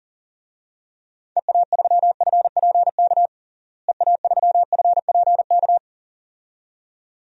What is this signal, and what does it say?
Morse code, heard as a keyed tone, about 37 wpm, 700 Hz. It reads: EA3FPK EA3FPK
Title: DXCC-100 Call Signs - 6 Characters - Repeated 2x - 40wpm